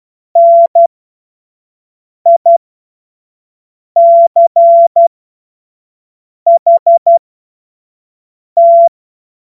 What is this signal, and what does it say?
Morse code, heard as a keyed tone, about 12 wpm, spaced as for 5 wpm, 680 Hz. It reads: NICHT
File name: Morsekurs / Witze / Witz 07. Juni 2022 25 F70